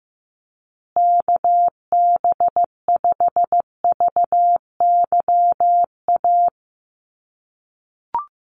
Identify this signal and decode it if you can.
Morse code, heard as a keyed tone, about 15 wpm, 700 Hz. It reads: KB5VYA
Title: USA Call Signs Repeated 2x 15wpm